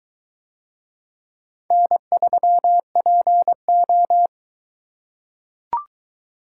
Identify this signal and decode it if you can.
Morse code, heard as a keyed tone, about 22 wpm, 700 Hz. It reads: N3PO